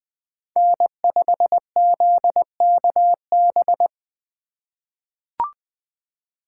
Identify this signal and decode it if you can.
Morse code, heard as a keyed tone, about 21 wpm, 700 Hz. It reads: N5ZKB
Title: USA Call Signs - 5 Characters - 2x Character Spacing - 20wpm